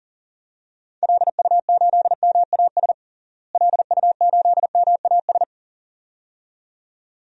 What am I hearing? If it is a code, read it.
Morse code, heard as a keyed tone, about 40 wpm, 700 Hz. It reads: LU8MAS LU8MAS
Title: DXCC-100 Call Signs - 6 Characters - Repeated 2x - 40wpm